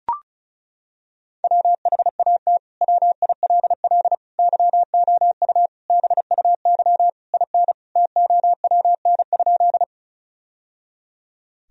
Morse code, heard as a keyed tone, about 36 wpm, 700 Hz. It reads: WHAT WILL YOU BUY IN TOWN?
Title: Sentences from Top 1000 Words - Review Part 4 - 35wpm